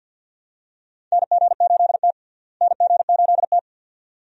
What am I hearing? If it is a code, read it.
Morse code, heard as a keyed tone, about 51 wpm, 700 Hz. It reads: NG8T NG8T
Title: USA Call Signs - 4 Characters - Repeated 2x - 50wpm